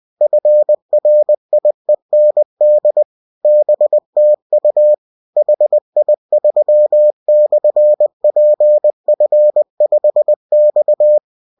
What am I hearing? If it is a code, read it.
Morse code, heard as a keyed tone, about 20 wpm, 600 Hz. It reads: FRIEND BTU HI3/PF5X